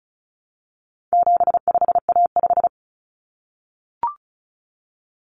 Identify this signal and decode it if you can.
Morse code, heard as a keyed tone, about 34 wpm, 700 Hz. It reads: 75A5
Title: Common Cut Numbers - 4-Digit 35wpm